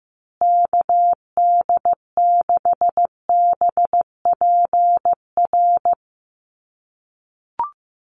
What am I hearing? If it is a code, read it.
Morse code, heard as a keyed tone, about 14 wpm, 700 Hz. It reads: KD6BPR
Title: USA Call Signs Repeated 2x 15wpm